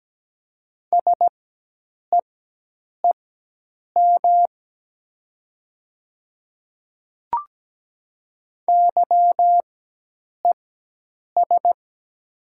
Morse code, heard as a keyed tone, about 17 wpm, spaced as for 8 wpm, 700 Hz. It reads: SEEM YES
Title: 4x Character Spacing - Top 500 Words - 17wpm